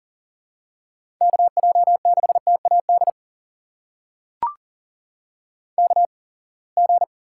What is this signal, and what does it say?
Morse code, heard as a keyed tone, about 40 wpm, spaced as for 28 wpm, 700 Hz. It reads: KJ6TAD KG
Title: USA Call Signs - 6 Characters - 8x Character Spacing - 40wpm